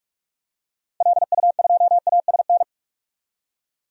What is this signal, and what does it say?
Morse code, heard as a keyed tone, about 42 wpm, 700 Hz. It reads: LU2ASN
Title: DXCC-100 Call Signs - 6 Characters - 45wpm